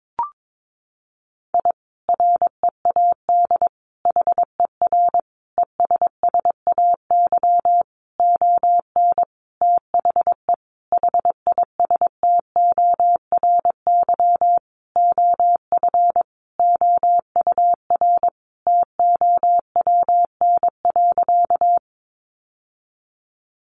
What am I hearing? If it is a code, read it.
Morse code, heard as a keyed tone, about 22 wpm, 700 Hz. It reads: I READ HER ESSAY ON THE HISTORY OF OUR TOWN.
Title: Sentences from Top 1500 Words 22wpm